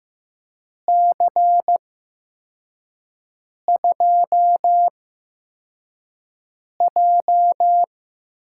Morse code, heard as a keyed tone, about 15 wpm, spaced as for 4 wpm, 700 Hz. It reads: C2J
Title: USA Call Signs - 8x Character Spacing - 15wpm